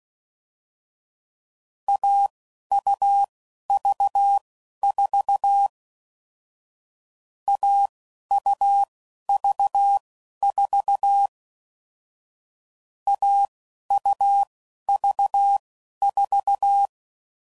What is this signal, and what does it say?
Morse code, heard as a keyed tone, about 16 wpm, spaced as for 11 wpm, 800 Hz. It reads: AUV4 AUV4 AUV4
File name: Mixed / Short Runs / AUV AUV4